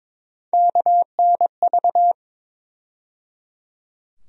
Morse code, heard as a keyed tone, about 22 wpm, 700 Hz. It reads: KNV
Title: USA Call Sign Postfixes - 3 Letters 22wpm